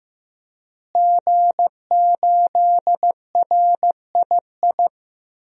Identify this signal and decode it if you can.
Morse code, heard as a keyed tone, about 15 wpm, 700 Hz. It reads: G8RII
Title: DXCC-100 Call Signs - 5 Characters - Repeated 3x - 15wpm